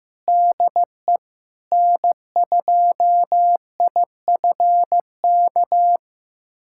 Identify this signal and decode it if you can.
Morse code, heard as a keyed tone, about 15 wpm, 700 Hz. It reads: DE N2IFK
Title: USA Callsign DE Callsign 15wpm